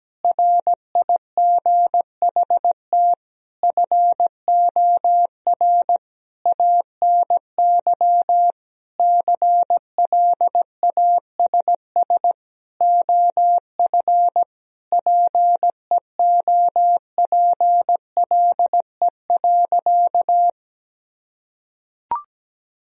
Morse code, heard as a keyed tone, about 17 wpm, 700 Hz. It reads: RIGHT FOR ANY CLASS OF PEOPLE.
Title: Sentences from Top 1000 Words - Review Part 2 - 17wpm